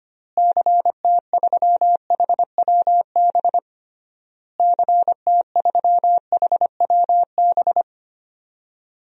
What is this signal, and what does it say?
Morse code, heard as a keyed tone, about 25 wpm, 700 Hz. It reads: CT3HWB CT3HWB
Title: DXCC-100 Call Signs - 6 Characters - Repeated 3x - 25wpm